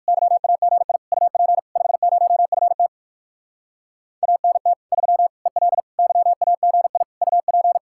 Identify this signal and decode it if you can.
Morse code, heard as a keyed tone, about 59 wpm, 700 Hz. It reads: YAGI UP 50FT ANT 3 EL YAGI UP